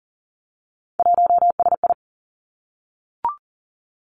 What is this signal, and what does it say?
Morse code, heard as a keyed tone, about 40 wpm, 700 Hz. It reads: 1SI
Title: Three Challenging Letter-Number Combinations 40wpm